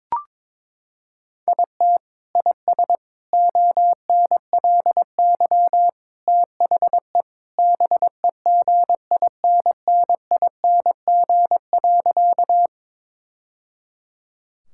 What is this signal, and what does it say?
Morse code, heard as a keyed tone, about 22 wpm, 700 Hz. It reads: IT IS ONLY THE BEGINNING.
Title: Sentences from Top 1000 Words - Review Part 5 - 22wpm